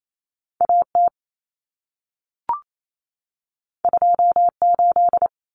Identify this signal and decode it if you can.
Morse code, heard as a keyed tone, about 28 wpm, 700 Hz. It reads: AT 28
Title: Common Cut Numbers - 2-Digit 28wpm